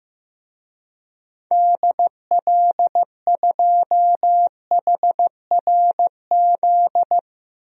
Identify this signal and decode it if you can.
Morse code, heard as a keyed tone, about 16 wpm, 700 Hz. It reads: DL2HRZ
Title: DXCC-100 Call Signs - 6 Characters - Repeated 3x - 15wpm